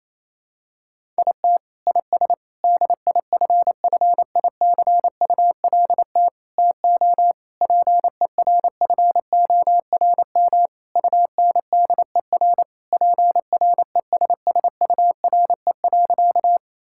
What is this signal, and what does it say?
Morse code, heard as a keyed tone, about 28 wpm, 700 Hz. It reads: IT IS DIFFICULT TO PERFORM UNDER PRESSURE.
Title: Sentences from Top 1000 Words - Review Part 6 - 28wpm